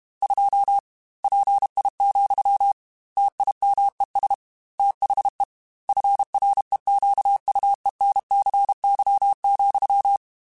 Morse code, heard as a keyed tone, about 32 wpm, 800 Hz. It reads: 2 PI, TIMES THE FREQUENCY,
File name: Technical Subjects / Technical Subjects-mp3s / XC @30